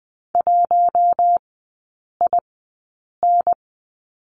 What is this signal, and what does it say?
Morse code, heard as a keyed tone, about 20 wpm, spaced as for 8 wpm, 700 Hz. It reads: 1IN
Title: 053 Lesson 10e - Letter r - Three Characters 20wpm